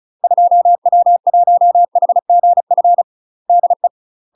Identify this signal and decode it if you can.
Morse code, heard as a keyed tone, about 34 wpm, 700 Hz. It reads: 2W1HGF DE
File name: UK MASTER QSOs TRAINING / 35WPM / UK MASTER QSOs 0024